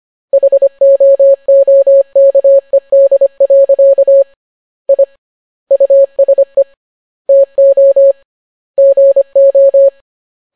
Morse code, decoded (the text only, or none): HOOKED. I USE TO GO